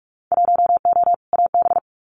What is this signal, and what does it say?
Morse code, heard as a keyed tone, about 46 wpm, 690 Hz. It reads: 1O AB
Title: Winter Field Day Exchanges Encore 45wpm